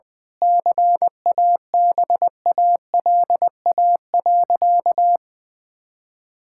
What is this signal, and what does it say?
Morse code, heard as a keyed tone, about 20 wpm, 700 Hz. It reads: CABALA.
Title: Pangram Challenge v2 - 20wpm